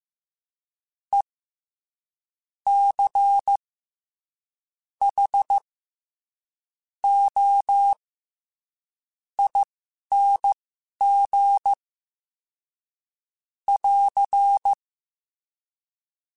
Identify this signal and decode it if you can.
Morse code, heard as a keyed tone, about 15 wpm, spaced as for 6 wpm, 800 Hz. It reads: ECHOING <AR>
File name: Special INGs-mp3s / Special INGs-mp3s / SlowToFaster ing @10